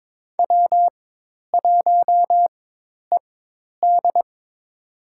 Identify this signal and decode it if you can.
Morse code, heard as a keyed tone, about 22 wpm, spaced as for 10 wpm, 700 Hz. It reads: W1ED
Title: USA Call Signs - 4 Characters - 4x Character Spacing - 22wpm